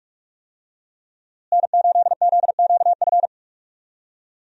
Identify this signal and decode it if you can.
Morse code, heard as a keyed tone, about 42 wpm, 700 Hz. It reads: N8ZQF
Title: USA Call Signs - 5 Characters - Repeated 3x - 45wpm